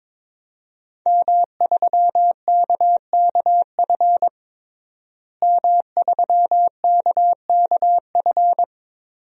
Morse code, heard as a keyed tone, about 22 wpm, 700 Hz. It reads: M3KKF M3KKF
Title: DXCC-100 Call Signs - 5 Characters - Repeated 3x - 22wpm